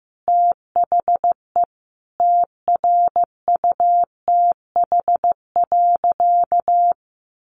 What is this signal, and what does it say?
Morse code, heard as a keyed tone, about 15 wpm, 700 Hz. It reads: THE TRUTH.
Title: Sentences from Top 1300 Words 15wpm